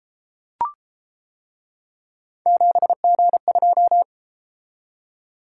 Morse code, heard as a keyed tone, about 29 wpm, 700 Hz. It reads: 7G2